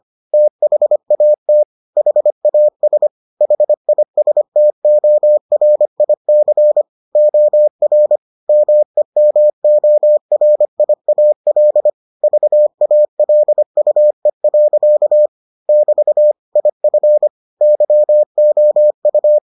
Morse code, heard as a keyed tone, about 25 wpm, 600 Hz. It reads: THAT HAS HISTORIC OR MEMORIAL VALUE. <BT> IF YOU